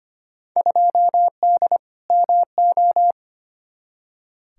Morse code, heard as a keyed tone, about 25 wpm, 700 Hz. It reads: 2D MO
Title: Field Day Exchanges - 25wpm